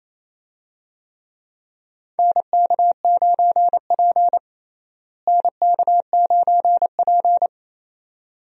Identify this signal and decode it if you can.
Morse code, heard as a keyed tone, about 28 wpm, 700 Hz. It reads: NK9P NK9P
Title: USA Call Signs - 4 Characters - Repeated 2x - 28wpm